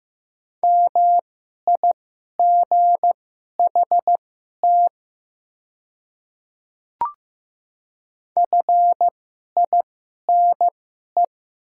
Morse code, heard as a keyed tone, about 15 wpm, spaced as for 11 wpm, 700 Hz. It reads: MIGHT FINE
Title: 2x Character Spacing - Top 500 Words - 15wpm